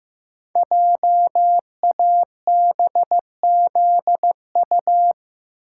Morse code, heard as a keyed tone, about 15 wpm, 700 Hz. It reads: JABZU
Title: Code Groups - 5 Letters - 15wpm